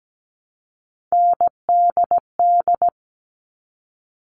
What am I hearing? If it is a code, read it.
Morse code, heard as a keyed tone, about 17 wpm, 700 Hz. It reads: NDD
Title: All Cut Numbers - 3-Digit 17wpm